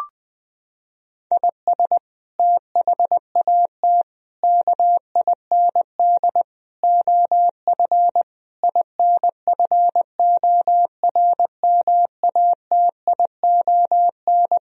Morse code, heard as a keyed tone, about 20 wpm, 700 Hz. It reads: IS THAT KIND OF INFORMATION
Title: Sentences from Top 1000 Words - Review Part 3 - 20wpm